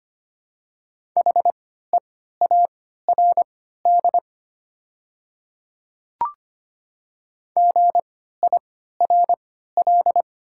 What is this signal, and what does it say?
Morse code, heard as a keyed tone, about 25 wpm, spaced as for 14 wpm, 700 Hz. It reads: HEARD GIRL
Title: 3x Character Spacing - Top 500 Words - 25wpm